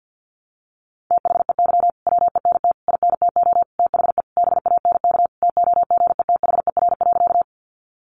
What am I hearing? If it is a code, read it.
Morse code, heard as a keyed tone, about 50 wpm, 700 Hz. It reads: THEY WENT INTO THE BANK TOGETHER.
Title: Sentences from Top 900 Words 50wpm